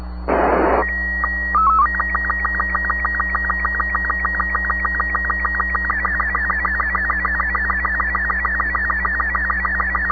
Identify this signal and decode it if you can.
SSTV (slow-scan television), Robot 36.
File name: sat / AO-51 14Jul2010 SSTV